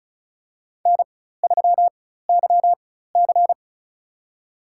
Morse code, heard as a keyed tone, about 34 wpm, spaced as for 16 wpm, 700 Hz. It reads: N3YC
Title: USA Call Signs - 4x Character Spacing - 35wpm